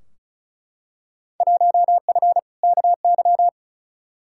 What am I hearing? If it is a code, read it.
Morse code, heard as a keyed tone, about 36 wpm, 690 Hz. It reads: 1F KY
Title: Field Day Exchanges Encore - 35wpm